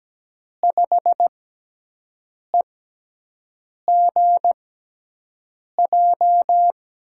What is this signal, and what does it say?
Morse code, heard as a keyed tone, about 17 wpm, spaced as for 6 wpm, 700 Hz. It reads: 5EGJ
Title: USA Call Signs - 6 Characters - 6x Character Spacing - 17wpm